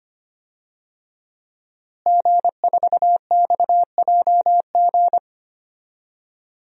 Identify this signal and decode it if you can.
Morse code, heard as a keyed tone, about 25 wpm, 700 Hz. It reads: G4XJG
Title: DXCC-100 Call Signs - 5 Characters - Repeated 3x - 25wpm